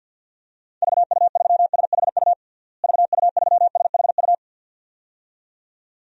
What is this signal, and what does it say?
Morse code, heard as a keyed tone, about 51 wpm, 700 Hz. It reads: VU3SHU VU3SHU
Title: DXCC-100 Call Signs - 6 Characters - Repeated 2x - 50wpm